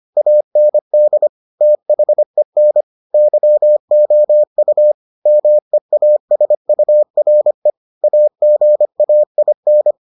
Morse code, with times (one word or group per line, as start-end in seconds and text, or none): 0.2-1.3 AND
1.6-2.8 THEN
3.1-4.9 YOU
5.3-7.7 MEASURE
8.0-9.9 AGAIN